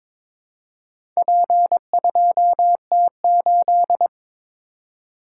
Morse code, heard as a keyed tone, about 22 wpm, 700 Hz. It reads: P2T8
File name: practice / 22wpm / Four Letter-Number Combinations - 22wpm